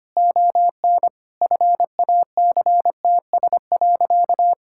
Morse code, heard as a keyed tone, about 25 wpm, 700 Hz. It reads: ON FACTS.
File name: practice / 25wpm / Sentences with 2-Word Phrases - 25wpm